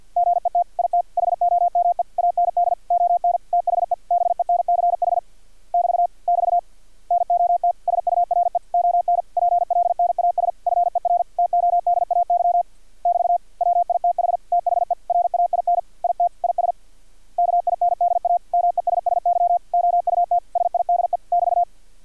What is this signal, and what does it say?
Morse code, heard as a keyed tone, about 50 wpm, 700 Hz. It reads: GET IT HOME AND ON THE BENCH <BT> <BT> NOT SURE ON PLNAS FER TODAY <BT> WITH THE RAIN IT IS KINDA MESSY OUT SIDE <BT>
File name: archived / CRSnet